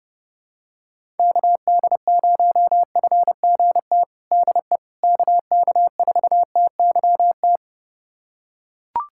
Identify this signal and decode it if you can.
Morse code, heard as a keyed tone, about 32 wpm, 700 Hz. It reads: KD0FGT DE KK4TYT
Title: USA Callsign DE Callsign - 6 Characters - 30wpm